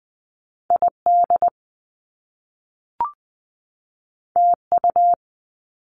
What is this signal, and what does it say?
Morse code, heard as a keyed tone, about 20 wpm, 700 Hz. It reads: ID TU
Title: Top Two Letter Combinations in Words 20wpm